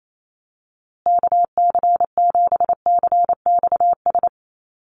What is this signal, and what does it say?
Morse code, heard as a keyed tone, about 28 wpm, 700 Hz. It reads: KC7CXS
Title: USA Call Signs Repeated 3x 28wpm